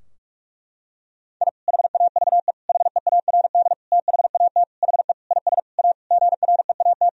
Morse code, heard as a keyed tone, about 41 wpm, 700 Hz. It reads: I HAVE HEARD THAT HE IS A GREAT
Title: Sentences from Top 1000 Words - Review Part 3 - 45wpm